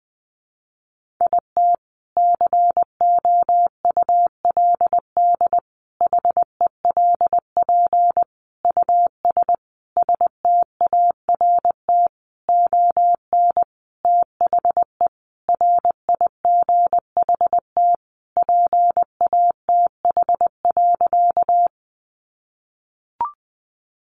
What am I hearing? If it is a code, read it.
Morse code, heard as a keyed tone, about 20 wpm, 700 Hz. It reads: IT COULD HELP US START ON THE RIGHT PATH.
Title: Sentences from Top 1400 Words 20wpm